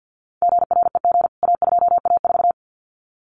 Morse code, heard as a keyed tone, about 50 wpm, 700 Hz. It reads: GREG A2A4
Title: CWT Exchange 50wpm